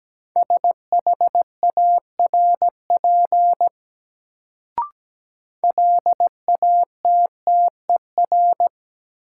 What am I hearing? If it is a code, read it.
Morse code, heard as a keyed tone, about 17 wpm, 700 Hz. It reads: SHARP LATTER
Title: Sentences from Top 1600 Words - New Word Drill - 17wpm